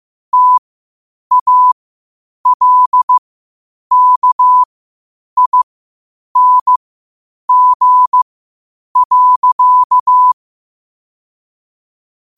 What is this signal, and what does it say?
Morse code, heard as a keyed tone, about 15 wpm, spaced as for 8 wpm, 1000 Hz. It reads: TALKING.
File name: qotdfiles / 8wpm